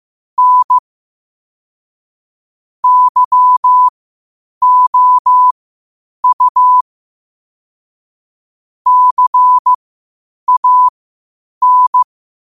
Morse code, heard as a keyed tone, about 15 wpm, spaced as for 8 wpm, 1000 Hz. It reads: N YOU CAN
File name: qotdfiles / 8wpm